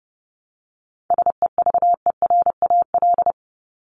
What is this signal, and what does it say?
Morse code, heard as a keyed tone, about 30 wpm, 700 Hz. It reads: SEVERAL
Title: Top 500 Words 30wpm